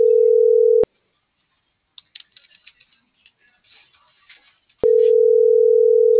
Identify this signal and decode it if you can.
Ringback tone (North America), 440 + 480 Hz.